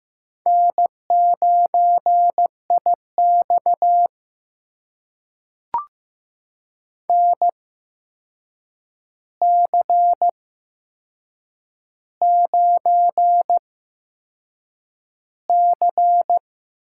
Morse code, heard as a keyed tone, about 15 wpm, spaced as for 6 wpm, 700 Hz. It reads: N9IX NC9C
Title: USA Call Signs - 4 Characters - 8x Character Spacing - 15wpm